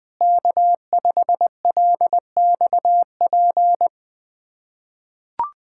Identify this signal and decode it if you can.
Morse code, heard as a keyed tone, about 20 wpm, 700 Hz. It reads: K5LXP
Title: USA Callsign DE Callsign - 5 Characters - 20wpm